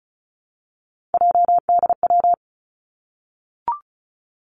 Morse code, heard as a keyed tone, about 34 wpm, 700 Hz. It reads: JDW
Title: Three Challenging Letter-Number Combinations 35wpm